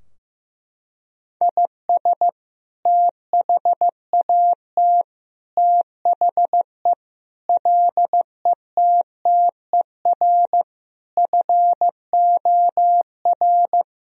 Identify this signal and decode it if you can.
Morse code, heard as a keyed tone, about 15 wpm, 700 Hz. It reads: IS THAT THE LETTER FOR
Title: Sentences from Top 1000 Words - Review Part 2 - 15wpm